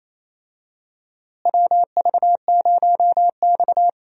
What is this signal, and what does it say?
Morse code, heard as a keyed tone, about 28 wpm, 700 Hz. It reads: WV0X
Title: USA Call Signs - 4 Characters - Repeated 2x - 28wpm